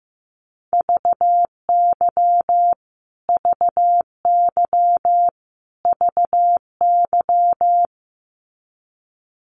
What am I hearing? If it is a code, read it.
Morse code, heard as a keyed tone, about 15 wpm, 700 Hz. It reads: VY VY VY
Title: Abbreviations Repeated 3x - 15wpm